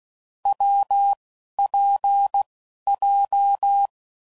Morse code, heard as a keyed tone, about 16 wpm, 800 Hz. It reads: W P J
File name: Combinations / Difficult Combos / WPJ WPJ1